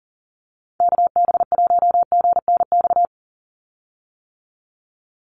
Morse code, heard as a keyed tone, about 40 wpm, 700 Hz. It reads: KB1GNX
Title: USA Call Signs 40wpm